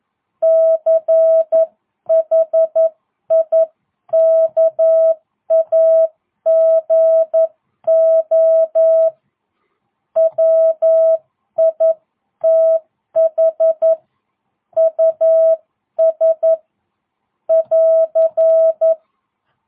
Morse code, heard as a keyed tone, about 11 wpm, 650 Hz. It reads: CHIKAGO WITH US <AR>